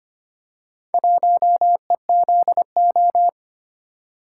Morse code, heard as a keyed tone, about 25 wpm, 700 Hz. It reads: 1EZO
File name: practice / 25wpm / Four Letter-Number Combinations - 25wpm